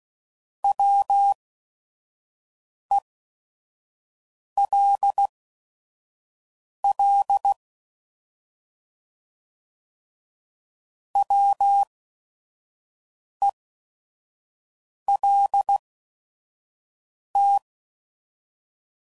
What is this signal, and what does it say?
Morse code, heard as a keyed tone, about 16 wpm, spaced as for 5 wpm, 800 Hz. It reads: WELL WELT